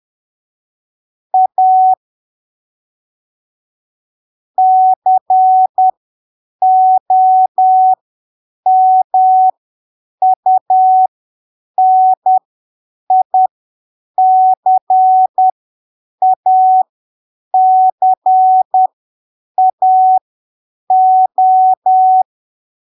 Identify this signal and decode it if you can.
Morse code, heard as a keyed tone, about 10 wpm, spaced as for 7 wpm, 750 Hz. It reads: A COMUNICACAO
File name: arquivos / texto / texto 18 10ppm